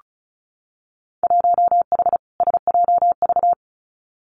Morse code, heard as a keyed tone, about 36 wpm, 700 Hz. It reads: 1H SJV
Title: Winter Field Day Exchanges 35wpm